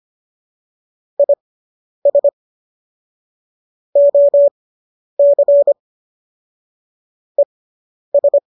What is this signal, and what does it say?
Morse code, heard as a keyed tone, about 25 wpm, spaced as for 10 wpm, 570 Hz. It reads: IS OC ES